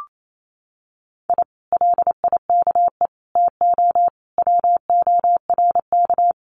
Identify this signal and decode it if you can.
Morse code, heard as a keyed tone, about 28 wpm, 700 Hz. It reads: I LIKE TO WORK
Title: Common QSO Phrases Encore 28wpm